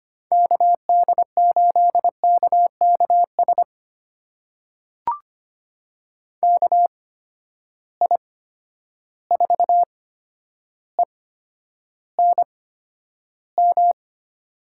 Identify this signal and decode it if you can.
Morse code, heard as a keyed tone, about 25 wpm, spaced as for 11 wpm, 700 Hz. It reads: KD8KKS KI4ENM
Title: USA Call Signs - 6 Characters - 8x Character Spacing - 25wpm